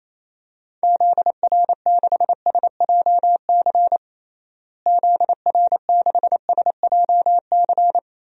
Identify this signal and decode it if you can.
Morse code, heard as a keyed tone, about 28 wpm, 700 Hz. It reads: ZR6SJC ZR6SJC
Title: DXCC-100 Call Signs - 6 Characters - Repeated 3x - 28wpm